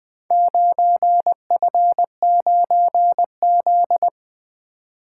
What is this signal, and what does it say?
Morse code, heard as a keyed tone, about 23 wpm, 700 Hz. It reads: 9F9Z
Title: Speed-Racing - Four Letter-Number Combinations - 20wpm